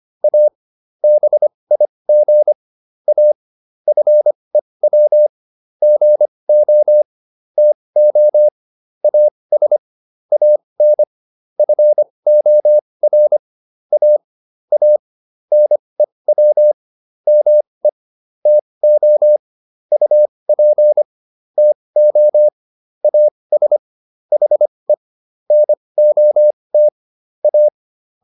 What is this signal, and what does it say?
Morse code, heard as a keyed tone, about 25 wpm, spaced as for 20 wpm, 600 Hz. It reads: A BIG A FEW GO TO AS AN FOR A A NEW ME TO UP TO AS HE NOT A